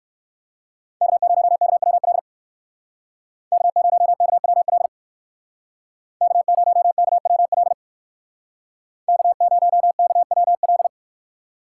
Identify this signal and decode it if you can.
Morse code, heard as a keyed tone, about 66 wpm, 700 Hz. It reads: K0KWL K0KWL K0KWL K0KWL